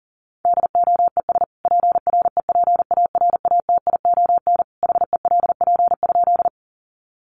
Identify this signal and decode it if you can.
Morse code, heard as a keyed tone, about 40 wpm, 700 Hz. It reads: DOES PREPARATION HELP?
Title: Sentences from Top 1400 Words 40wpm